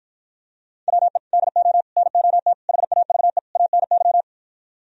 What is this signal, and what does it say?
Morse code, heard as a keyed tone, about 52 wpm, 700 Hz. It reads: WE DO NOT HAVE ANY